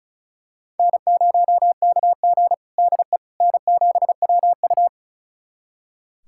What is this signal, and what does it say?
Morse code, heard as a keyed tone, about 36 wpm, 700 Hz. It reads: N0KG DE N7WU
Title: USA Callsign DE Callsign - 4 Characters - 35wpm